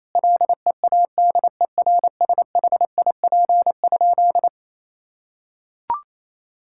Morse code, heard as a keyed tone, about 28 wpm, 700 Hz. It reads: LEADERSHIP?
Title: Sentences from Top 1600 Words 28wpm